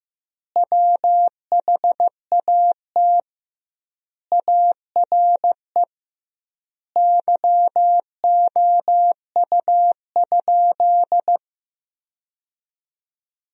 Morse code, heard as a keyed tone, about 15 wpm, 700 Hz. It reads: WHAT ARE YOU?
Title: 2x Word-Spacing - Sentences from Top 100 Words - 15wpm